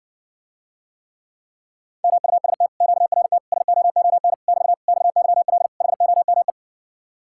Morse code, heard as a keyed tone, about 70 wpm, 700 Hz. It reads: MUST QRT SOON <BT> XYL HOME